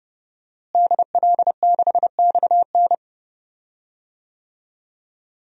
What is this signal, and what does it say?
Morse code, heard as a keyed tone, about 30 wpm, 700 Hz. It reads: DL6XN
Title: DXCC-100 Call Signs - 5 Characters - Repeated 2x - 30wpm